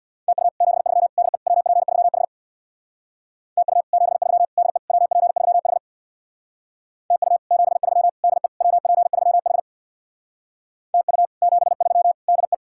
Morse code, heard as a keyed tone, about 65 wpm, 700 Hz. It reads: TU 73 DE WP3H TU 73 DE WP3H TU 73 DE WP3H TU 73 DE